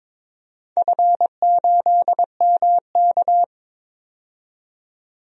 Morse code, heard as a keyed tone, about 22 wpm, 700 Hz. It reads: F8MK
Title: DXCC-100 Call Signs - 4 Characters - 22wpm